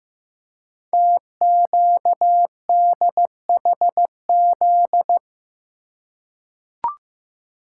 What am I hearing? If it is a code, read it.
Morse code, heard as a keyed tone, about 15 wpm, 700 Hz. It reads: TQDHZ
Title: Code Groups - 5 Letter:Numbers - 15wpm